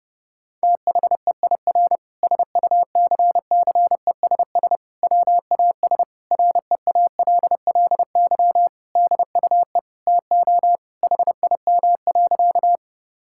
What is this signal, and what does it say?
Morse code, heard as a keyed tone, about 30 wpm, 700 Hz. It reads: THEIR SUCCESS WAS REALLY DUE TO HIM.
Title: Sentences from Top 1000 Words - Review Part 6 - 30wpm